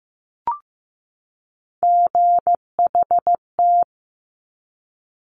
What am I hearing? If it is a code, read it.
Morse code, heard as a keyed tone, about 15 wpm, 700 Hz. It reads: GHT
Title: Top Three Letter Combinations in Words 15wpm